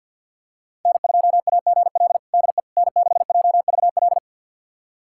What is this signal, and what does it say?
Morse code, heard as a keyed tone, about 51 wpm, 690 Hz. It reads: N2AGL DE N6JVL
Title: USA Callsign DE Callsign - 5 Characters - 50wpm